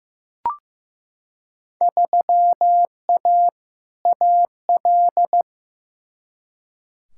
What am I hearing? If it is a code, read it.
Morse code, heard as a keyed tone, about 15 wpm, 700 Hz. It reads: 3A AL
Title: Field Day Exchanges - 15wpm